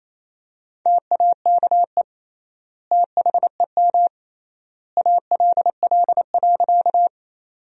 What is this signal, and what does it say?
Morse code, heard as a keyed tone, about 28 wpm, 700 Hz. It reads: TAKE THEM ALL.
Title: 3x Word-Spacing - Sentences from Top 200 Words - 28wpm